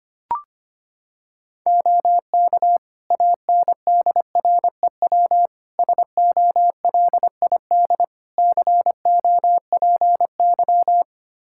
Morse code, heard as a keyed tone, about 25 wpm, 700 Hz. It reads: OK ANDREW SOLID COPY
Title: Practice QSO - Exchange 3 - More Info - 25wpm